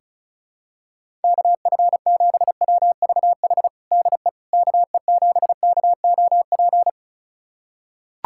Morse code, heard as a keyed tone, about 36 wpm, 700 Hz. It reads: KF7WVH DE KE7KOP
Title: USA Callsign DE Callsign - 6 Characters - 35wpm